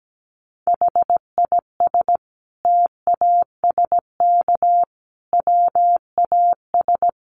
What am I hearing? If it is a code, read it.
Morse code, heard as a keyed tone, about 17 wpm, 700 Hz. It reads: HIS TASK WAS
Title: Sentences from Top 1200 Words 17wpm